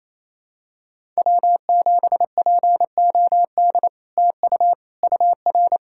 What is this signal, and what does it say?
Morse code, heard as a keyed tone, about 29 wpm, 700 Hz. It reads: W7POD TU UR
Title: SOTA QSO Practice 28wpm